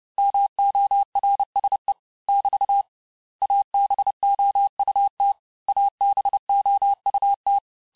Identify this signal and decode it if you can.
Morse code, heard as a keyed tone, about 29 wpm, 800 Hz. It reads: MORSE <BT> ABOUT ABOUT
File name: cours-du-mercredi / audios / cours-854-mots-vitesse-28-mpm